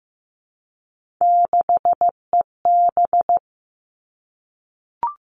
Morse code, heard as a keyed tone, about 15 wpm, 700 Hz. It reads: 6EB
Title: Three Challenging Letter-Number Combinations 15wpm